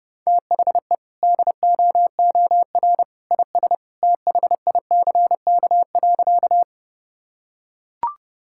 Morse code, heard as a keyed tone, about 32 wpm, 700 Hz. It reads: THE DOOR IS THICK.
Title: Sentences from Top 1600 Words 30wpm